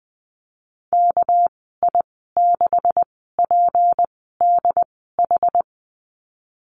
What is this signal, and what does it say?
Morse code, heard as a keyed tone, about 19 wpm, spaced as for 14 wpm, 700 Hz. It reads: KI6PDH
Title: 219 Lesson 35g - Number 0 - Call Signs 20wpm